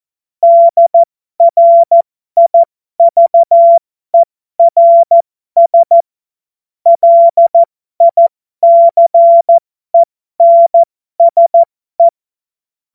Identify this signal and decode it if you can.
Morse code, heard as a keyed tone, about 14 wpm, spaced as for 12 wpm, 680 Hz. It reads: DRIVERS LICENSE
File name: Morsekurs / Witze